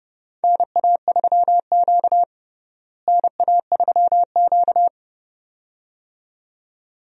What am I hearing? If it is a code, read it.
Morse code, heard as a keyed tone, about 31 wpm, 700 Hz. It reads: NA3Q NA3Q
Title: USA Call Signs - 4 Characters - Repeated 2x - 30wpm